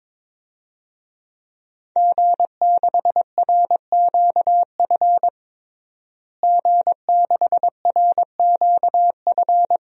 Morse code, heard as a keyed tone, about 22 wpm, 700 Hz. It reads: G6RQF G6RQF
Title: DXCC-100 Call Signs - 5 Characters - Repeated 3x - 22wpm